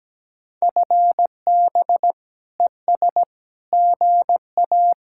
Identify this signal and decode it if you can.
Morse code, heard as a keyed tone, about 17 wpm, 700 Hz. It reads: FB ES GA
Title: Speed-Racing Common QSO Phrases Encore - 17wpm